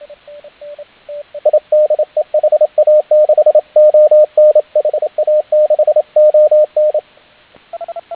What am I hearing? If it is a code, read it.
Morse code, heard as a keyed tone, about 29 wpm, 610 Hz, with radio noise. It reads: I DE HA6ON HA6ON